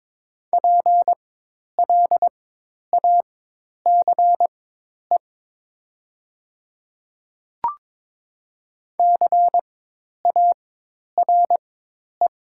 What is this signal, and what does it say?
Morse code, heard as a keyed tone, about 22 wpm, spaced as for 10 wpm, 700 Hz. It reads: PLACE CARE
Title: 4x Character Spacing - Top 500 Words - 22wpm